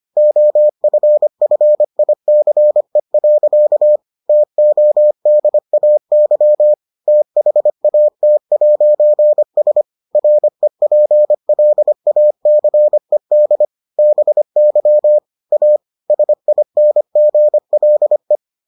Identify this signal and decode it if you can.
Morse code, heard as a keyed tone, about 25 wpm, 600 Hz. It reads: OFFICE. TODAY THAT'S REPLACED BY A SINGLE